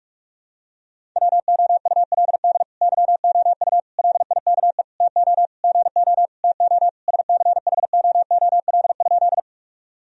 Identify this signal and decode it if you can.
Morse code, heard as a keyed tone, about 46 wpm, 700 Hz. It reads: WOULD YOU LIKE TO GO TO SCHOOL?
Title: Sentences from Top 1000 Words - Review Part 1 - 45wpm